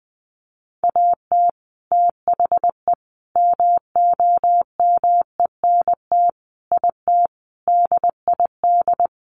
Morse code, heard as a keyed tone, about 20 wpm, 700 Hz. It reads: AT THE MOMENT IT DID
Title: Sentences from Top 1200 Words 20wpm